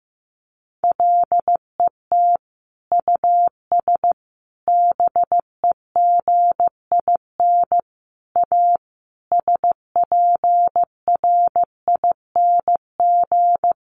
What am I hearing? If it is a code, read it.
Morse code, heard as a keyed tone, about 15 wpm, 700 Hz. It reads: LET US BEGIN A SPRING
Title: Sentences from Top 1400 Words 15wpm